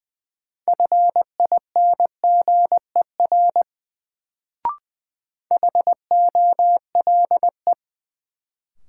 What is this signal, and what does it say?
Morse code, heard as a keyed tone, about 21 wpm, 700 Hz. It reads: FINGER HOLE
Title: Sentences from Top 1600 Words - New Word Drill - 20wpm